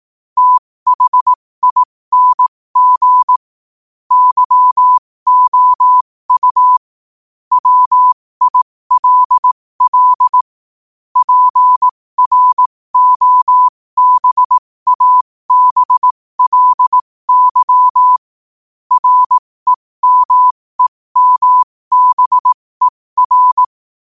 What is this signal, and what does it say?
Morse code, heard as a keyed tone, about 18 wpm, spaced as for 15 wpm, 1000 Hz. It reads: THING YOU WILL PROBABLY REMEMBER
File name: qotdfiles / 15wpm